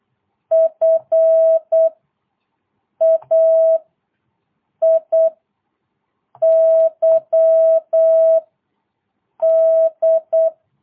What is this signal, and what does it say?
Morse code, heard as a keyed tone, about 8 wpm, 650 Hz. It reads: F A I Y D
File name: jh8cbh / 2022cw